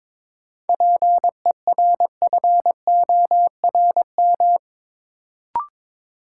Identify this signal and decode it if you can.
Morse code, heard as a keyed tone, about 22 wpm, 700 Hz. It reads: PERFORM
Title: Sentences from Top 1000 Words - Review Part 5 - New Word Drill - 22wpm